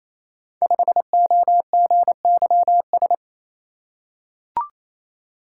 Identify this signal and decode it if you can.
Morse code, heard as a keyed tone, about 27 wpm, 700 Hz. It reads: 5OGYS
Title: Code Groups - 5 Letter:Numbers - 28wpm